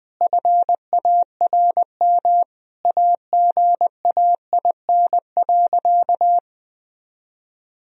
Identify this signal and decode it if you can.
Morse code, heard as a keyed tone, about 20 wpm, 700 Hz. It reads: FARM AGAIN.
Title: Sentences from Top 1000 Words - Review Part 2 - 20wpm